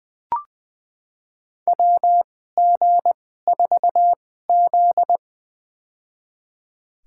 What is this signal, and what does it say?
Morse code, heard as a keyed tone, about 20 wpm, spaced as for 14 wpm, 700 Hz. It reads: WG4Z
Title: USA Call Signs - 4 Characters - 2x Character Spacing - 20wpm